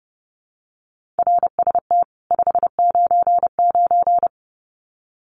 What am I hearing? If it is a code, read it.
Morse code, heard as a keyed tone, about 28 wpm, 700 Hz. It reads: RST 599
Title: Common QSO Phrases 30wpm